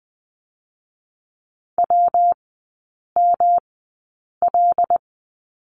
Morse code, heard as a keyed tone, about 20 wpm, spaced as for 8 wpm, 700 Hz. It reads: WML
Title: 138 Lesson 23e - Letter f - Three Characters 20wpm